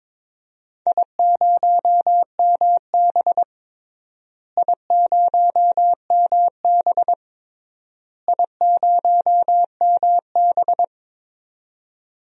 Morse code, heard as a keyed tone, about 21 wpm, 700 Hz. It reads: I0MB I0MB I0MB
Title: DXCC-100 Call Signs - 4 Characters - Repeated 3x - 22wpm